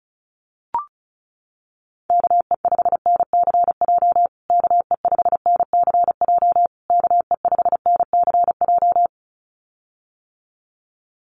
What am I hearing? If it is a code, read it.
Morse code, heard as a keyed tone, about 34 wpm, 700 Hz. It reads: KE5NCJ KE5NCJ KE5NCJ
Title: USA Call Signs Repeated 3x 35wpm